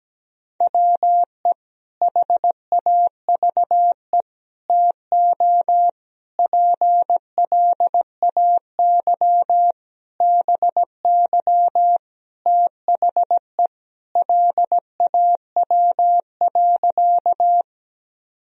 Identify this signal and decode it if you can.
Morse code, heard as a keyed tone, about 17 wpm, 700 Hz. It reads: WE HAVE TO PLAY BY THE LAW.
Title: Sentences from Top 1000 Words - Review Part 3 - 17wpm